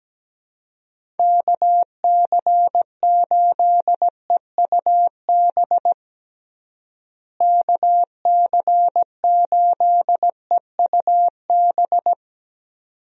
Morse code, heard as a keyed tone, about 17 wpm, 700 Hz. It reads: KC8EUB KC8EUB
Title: USA Call Signs - 6 Characters - Repeated 2x - 17wpm